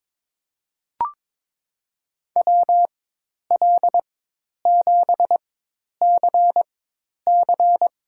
Morse code, heard as a keyed tone, about 22 wpm, spaced as for 10 wpm, 700 Hz. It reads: WL7CC
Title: USA Call Signs - 5 Characters - 4x Character Spacing - 22wpm